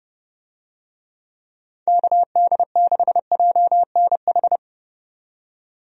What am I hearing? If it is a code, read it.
Morse code, heard as a keyed tone, about 30 wpm, 700 Hz. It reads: KD6JNH
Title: USA Call Signs - 6 Characters - 30wpm